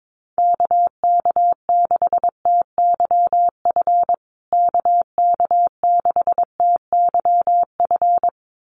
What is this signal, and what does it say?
Morse code, heard as a keyed tone, about 22 wpm, 700 Hz. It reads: KK6TYF KK6TYF
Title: USA Call Signs Repeated 2x 22wpm